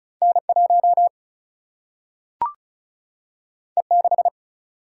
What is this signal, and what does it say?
Morse code, heard as a keyed tone, about 36 wpm, 700 Hz. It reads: N 1 E 6
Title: Single-Letter Single-Number 35wpm